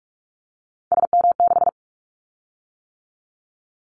Morse code, heard as a keyed tone, about 46 wpm, 700 Hz. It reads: SM6
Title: Callsign Prefixes for Top 25 DXCC - 45wpm